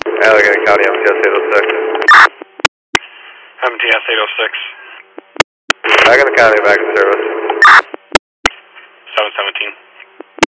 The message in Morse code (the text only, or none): none